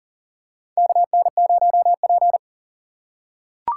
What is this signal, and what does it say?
Morse code, heard as a keyed tone, about 38 wpm, 700 Hz. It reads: KN0P
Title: USA Call Signs - 4 Characters - 8x Character Spacing - 40wpm